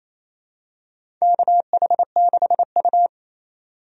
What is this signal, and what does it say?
Morse code, heard as a keyed tone, about 28 wpm, 700 Hz. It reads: KH6U
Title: DXCC-100 Call Signs - 4 Characters - 28wpm